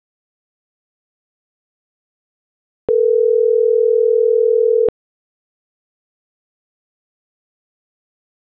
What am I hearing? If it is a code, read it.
Ringback tone (North America), 440 + 480 Hz.